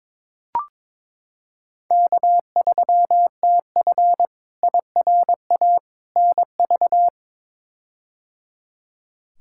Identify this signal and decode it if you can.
Morse code, heard as a keyed tone, about 21 wpm, 700 Hz. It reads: K3TF IRA NV
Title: K1USN SST QSO Practice - 22wpm